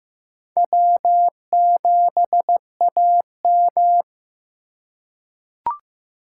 Morse code, heard as a keyed tone, about 16 wpm, 700 Hz. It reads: W7AM
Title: USA Call Signs - 4 Characters - 2x Character Spacing - 15wpm